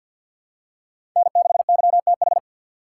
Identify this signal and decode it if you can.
Morse code, heard as a keyed tone, about 48 wpm, 700 Hz. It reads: N6YTH
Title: USA Call Signs - 3x Character Spacing - 50wpm